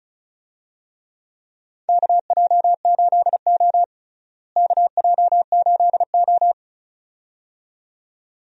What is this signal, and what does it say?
Morse code, heard as a keyed tone, about 36 wpm, 700 Hz. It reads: KJ8O KJ8O
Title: USA Call Signs - 4 Characters - Repeated 2x - 35wpm